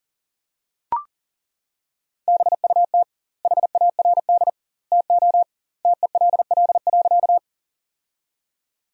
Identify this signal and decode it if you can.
Morse code, heard as a keyed tone, about 40 wpm, 700 Hz. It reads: BUT HARD TO TELL.
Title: 2x Word-Spacing - Sentences from Top 300 Words - 40wpm